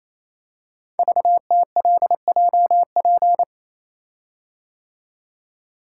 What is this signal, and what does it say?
Morse code, heard as a keyed tone, about 28 wpm, 690 Hz. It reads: VTLJP
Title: Code Groups - 5 Letters - 28wpm